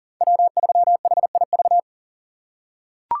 Morse code, heard as a keyed tone, about 40 wpm, 700 Hz. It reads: W3HIV
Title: USA Call Signs - 5 Characters - 6x Character Spacing - 40wpm